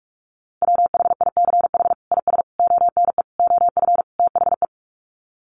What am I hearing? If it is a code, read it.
Morse code, heard as a keyed tone, about 44 wpm, 700 Hz. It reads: WHICH IS ONE OF THE
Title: Sets of 5 Words Encore 45wpm